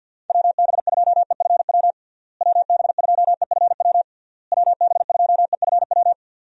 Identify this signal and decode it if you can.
Morse code, heard as a keyed tone, about 48 wpm, 700 Hz. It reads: WB2EFW WB2EFW WB2EFW
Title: USA Call Signs - 6 Characters - Repeated 3x - 50wpm